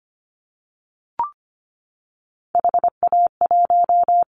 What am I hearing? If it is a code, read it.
Morse code, heard as a keyed tone, about 25 wpm, 700 Hz. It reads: HA1
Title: Callsign Prefixes for Top 25 DXCC - 25wpm